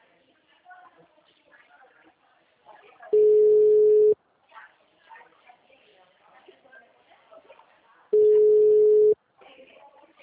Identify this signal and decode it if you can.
Ringback tone (Europe, 425 Hz), with speech.